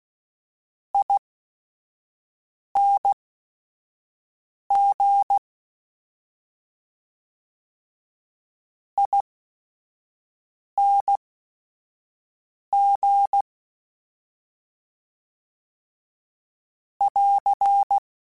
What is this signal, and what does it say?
Morse code, heard as a keyed tone, about 16 wpm, spaced as for 5 wpm, 800 Hz. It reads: ING ING <AR>